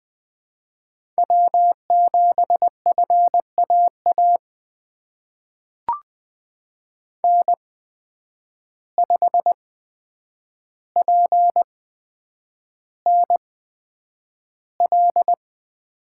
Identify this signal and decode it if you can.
Morse code, heard as a keyed tone, about 20 wpm, spaced as for 9 wpm, 700 Hz. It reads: W7FAA N5PNL
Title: USA Call Signs - 5 Characters - 8x Character Spacing - 20wpm